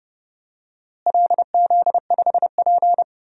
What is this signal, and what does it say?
Morse code, heard as a keyed tone, about 29 wpm, 700 Hz. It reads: LZ5P
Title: DXCC-100 Call Signs - 4 Characters - Repeated 2x - 30wpm